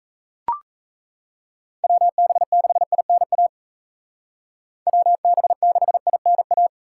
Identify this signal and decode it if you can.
Morse code, heard as a keyed tone, about 36 wpm, 700 Hz. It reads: WB6INA WB6INA